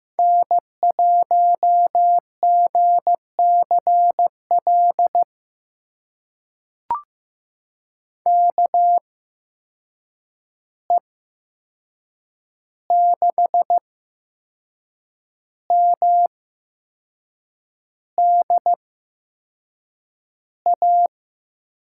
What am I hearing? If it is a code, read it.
Morse code, heard as a keyed tone, about 15 wpm, spaced as for 6 wpm, 700 Hz. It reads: N1GCL KE6MDA
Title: USA Call Signs - 8x Character Spacing - 15wpm